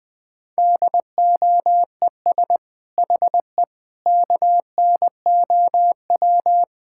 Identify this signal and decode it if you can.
Morse code, heard as a keyed tone, about 20 wpm, 700 Hz. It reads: DOES HE KNOW
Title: Sentences from Top 1000 Words - Review Part 6 - 20wpm